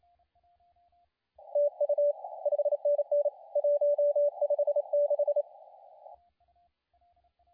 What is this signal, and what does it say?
Morse code, heard as a keyed tone, about 27 wpm, 580 Hz, with radio noise. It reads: TU 5NN 156
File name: la4c / sac-cw-2017 / CONTEST AUDIO 2100Z